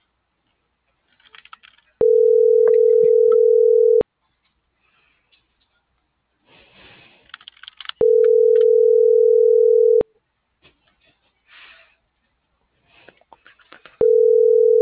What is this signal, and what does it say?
Ringback tone (North America), 440 + 480 Hz.